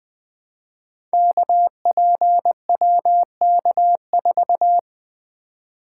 Morse code, heard as a keyed tone, about 20 wpm, 700 Hz. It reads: KPWK4
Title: Code Groups - 5 Letter:Numbers - 20wpm